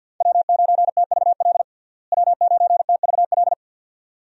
Morse code, heard as a keyed tone, about 53 wpm, 700 Hz. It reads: W9TVL W9TVL
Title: USA Call Signs - 5 Characters - Repeated 2x - 50wpm